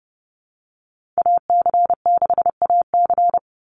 Morse code, heard as a keyed tone, about 30 wpm, 700 Hz. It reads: AC6AC
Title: Call Signs Repeated 3x - 30wpm